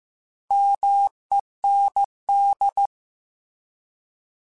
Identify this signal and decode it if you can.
Morse code, heard as a keyed tone, about 15 wpm, 800 Hz. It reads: MEND
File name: Alphabetical / Alphabetical mp3s / Ms @13